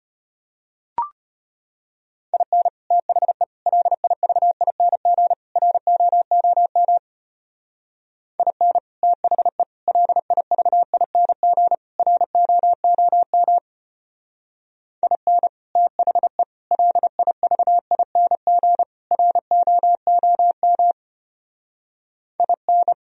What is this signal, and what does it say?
Morse code, heard as a keyed tone, about 31 wpm, 700 Hz. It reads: IN THE LIVING ROOM IN THE LIVING ROOM IN THE LIVING ROOM IN
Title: Speed-Racing - Sets of 4 Words 25wpm